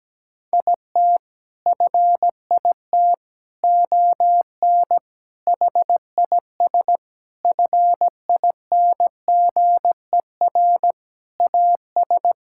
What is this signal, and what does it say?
Morse code, heard as a keyed tone, about 17 wpm, 700 Hz. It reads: IT FIT ON HIS FINGER AS
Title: Sentences from Top 1600 Words 17wpm